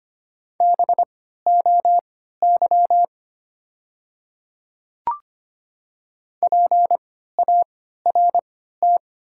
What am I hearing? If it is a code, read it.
Morse code, heard as a keyed tone, about 25 wpm, spaced as for 14 wpm, 700 Hz. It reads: BOY PART
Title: 3x Character Spacing - Top 500 Words - 25wpm